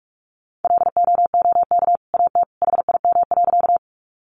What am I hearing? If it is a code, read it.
Morse code, heard as a keyed tone, about 44 wpm, 700 Hz. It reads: LOOK AT HIM.
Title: Sentences from Top 100 Words 45wpm